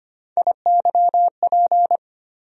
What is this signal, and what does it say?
Morse code, heard as a keyed tone, about 25 wpm, 700 Hz. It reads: IYP
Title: USA Call Sign Postfixes - 3 Letters 25wpm